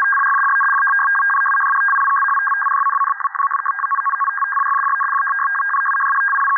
Two-tone teleprinter signal, about 500 Hz shift.